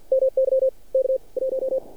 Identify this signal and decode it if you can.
Morse code, heard as a keyed tone, about 50 wpm, 530 Hz. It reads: MY K 1
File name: archived / CRSnet